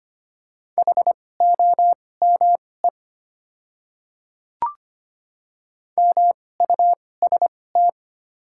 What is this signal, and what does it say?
Morse code, heard as a keyed tone, about 25 wpm, spaced as for 18 wpm, 700 Hz. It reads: HOME MUST
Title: 2x Character Spacing - Top 300 Words - 25wpm